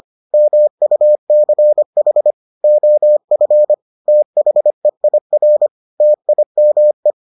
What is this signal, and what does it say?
Morse code, heard as a keyed tone, about 25 wpm, 600 Hz. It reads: MUCH OF THEIR TIME